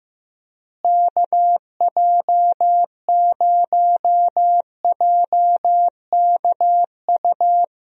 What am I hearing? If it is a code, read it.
Morse code, heard as a keyed tone, about 15 wpm, 700 Hz. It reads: KJ0JKU
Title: USA Call Signs - 6 Characters - Repeated 2x - 15wpm